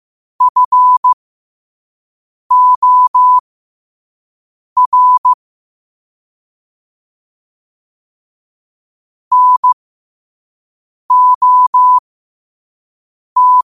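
Morse code, heard as a keyed tone, about 15 wpm, spaced as for 5 wpm, 1000 Hz. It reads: FOR NOT
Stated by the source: Morse practice at 5 wpm.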